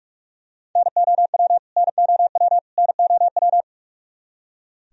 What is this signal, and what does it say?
Morse code, heard as a keyed tone, about 46 wpm, 700 Hz. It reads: NOW NOW NOW
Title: Top 100 Words Repeated 45wpm